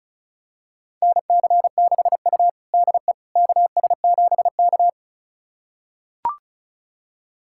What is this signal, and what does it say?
Morse code, heard as a keyed tone, about 36 wpm, 700 Hz. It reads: NC6U DE KS7K
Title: USA Callsign DE Callsign - 4 Characters - 35wpm